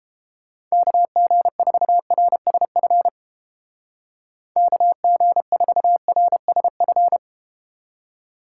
Morse code, heard as a keyed tone, about 30 wpm, 700 Hz. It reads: KG4RSF KG4RSF